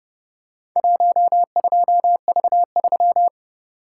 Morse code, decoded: 12V3

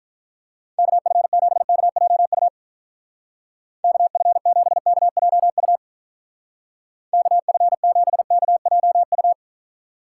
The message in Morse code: KF7KJU KF7KJU KF7KJU

